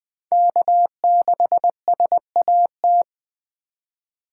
Morse code, heard as a keyed tone, about 23 wpm, 700 Hz. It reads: K6SAT
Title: Speed Racing - USA 1x3 Call Signs 20wpm